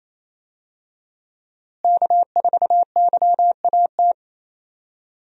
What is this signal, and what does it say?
Morse code, heard as a keyed tone, about 27 wpm, 700 Hz. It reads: K4YAT
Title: USA Call Signs - 5 Characters - Repeated 2x - 28wpm